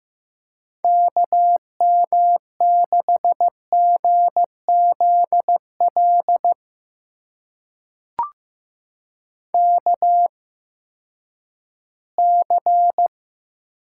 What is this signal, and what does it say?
Morse code, heard as a keyed tone, about 15 wpm, spaced as for 10 wpm, 700 Hz. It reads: KM6GZL KC
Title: USA Call Signs - 6 Characters - 8x Character Spacing - 15wpm